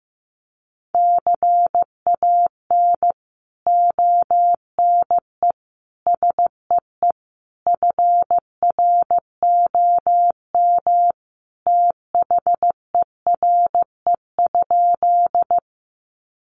Morse code, heard as a keyed tone, about 15 wpm, 700 Hz. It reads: CAN ONE SEE FROM THERE?
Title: Sentences from Top 100 Words 15wpm